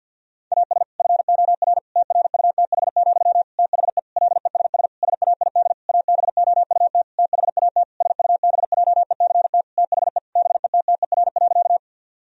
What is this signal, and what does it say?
Morse code, heard as a keyed tone, about 50 wpm, 700 Hz. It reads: AS FOR TRUTH, THE LESS SAID ABOUT THAT SUBJECT THE BETTER.
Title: Sentences from Top 1000 Words - Review Part 2 - 50wpm